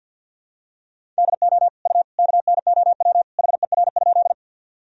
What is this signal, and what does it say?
Morse code, heard as a keyed tone, about 49 wpm, 700 Hz. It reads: DO U KNOW HER?
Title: Sentences from Top 100 Words and Abbreviations 50wpm